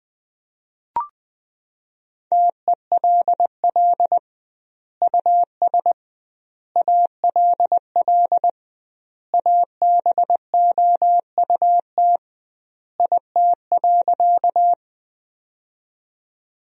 Morse code, heard as a keyed tone, about 20 wpm, 700 Hz. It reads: TELL US ALL ABOUT IT.
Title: 2x Word-Spacing - Sentences from Top 200 Words - 20wpm